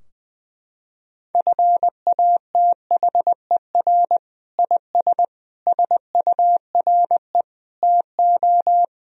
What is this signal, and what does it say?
Morse code, heard as a keyed tone, about 20 wpm, 700 Hz. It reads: FATHER IS SURE TO
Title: Sentences from Top 1000 Words - Review Part 3 - 20wpm